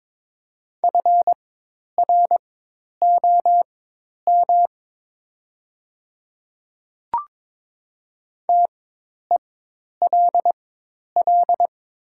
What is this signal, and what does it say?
Morse code, heard as a keyed tone, about 22 wpm, spaced as for 10 wpm, 700 Hz. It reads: FROM TELL
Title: 4x Character Spacing - Top 200 Words - 22wpm